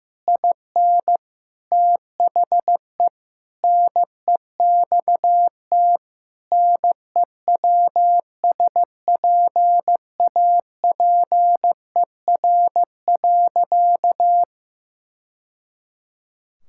Morse code, heard as a keyed tone, about 15 wpm, 700 Hz. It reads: IN THE NEXT NEWSPAPER.
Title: Sentences from Top 1000 Words - Review Part 6 - 15wpm